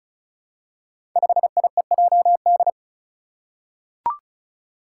Morse code, heard as a keyed tone, about 34 wpm, 700 Hz. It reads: 5IEJD
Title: Five Challenging Letter-Number Combinations 35wpm